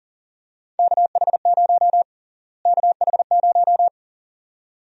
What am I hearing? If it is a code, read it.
Morse code, heard as a keyed tone, about 40 wpm, 700 Hz. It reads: KH0 KH0
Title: USA Call Sign Prefixes - Repeated 2x - 40wpm